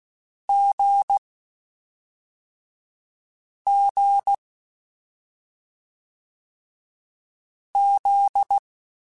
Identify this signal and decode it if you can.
Morse code, heard as a keyed tone, about 16 wpm, 800 Hz. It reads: G G Z
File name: Short Runs / Short Runs-mp3s / awj1ndb6gz7 @5